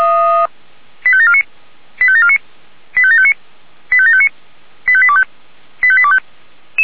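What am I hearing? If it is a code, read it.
Five-tone selective call (ZVEI), heard as quick runs of five tones, tones of about 70 ms. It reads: 85749 85749 85759 85759 85825 85825
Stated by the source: ZVEI five-tone selective call 85725, received by SDR.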